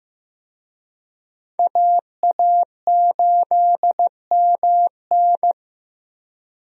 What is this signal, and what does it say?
Morse code, heard as a keyed tone, about 15 wpm, 700 Hz. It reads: AA8MN
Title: USA Call Signs - 5 Characters - Repeated 3x - 15wpm